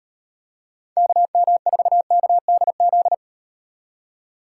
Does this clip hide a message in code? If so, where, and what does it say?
Yes: Morse code from 1.0 to 3.1 s, heard as a keyed tone, about 35 wpm, 700 Hz. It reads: KM4KDZ